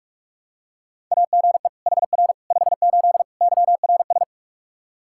Morse code, heard as a keyed tone, about 46 wpm, 700 Hz. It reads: AGE HR 58 YRS
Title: Practice QSO - Exchange 3 - More Info - 45wpm